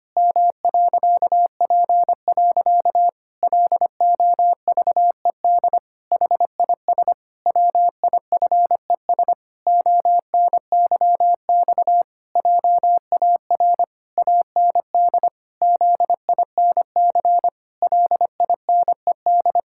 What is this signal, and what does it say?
Morse code, heard as a keyed tone, about 25 wpm, 700 Hz. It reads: M.P. LOVED HIS WIFES ONYX JAR AND ZINC LINED
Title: Pangram Challenge v2 - 25wpm